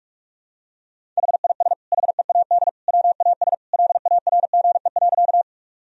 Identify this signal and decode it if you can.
Morse code, heard as a keyed tone, about 46 wpm, 700 Hz. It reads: HIS HEAD WAS LARGE.
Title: Sentences from Top 1000 Words - Review Part 1 - 45wpm